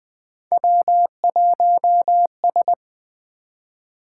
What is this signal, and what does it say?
Morse code, heard as a keyed tone, about 20 wpm, 700 Hz. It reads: W1S
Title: Three Letter-Number Combinations - 20wpm